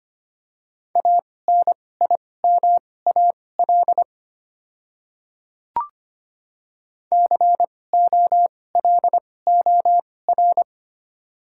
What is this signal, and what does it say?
Morse code, heard as a keyed tone, about 25 wpm, spaced as for 18 wpm, 700 Hz. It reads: ANIMAL COLOR
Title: 2x Character Spacing - Top 300 Words - 25wpm